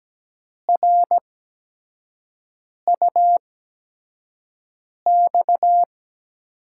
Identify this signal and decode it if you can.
Morse code, heard as a keyed tone, about 17 wpm, spaced as for 5 wpm, 700 Hz. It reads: RUX
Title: USA Call Signs - 6 Characters - 8x Character Spacing - 17wpm